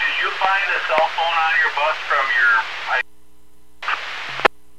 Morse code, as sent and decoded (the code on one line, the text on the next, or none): none